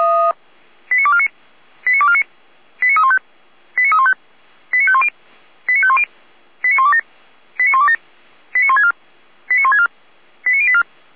Five-tone selective call (ZVEI), heard as quick runs of five tones, tones of about 70 ms. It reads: 89249 89249 89315 89315 89510 89510 89127 89127 89164 89164 89084